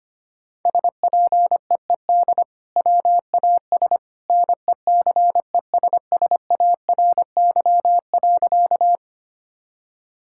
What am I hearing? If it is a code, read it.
Morse code, heard as a keyed tone, about 25 wpm, 700 Hz. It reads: SPEED WAS NECESSARY.
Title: Sentences from Top 1000 Words - Review Part 5 - 25wpm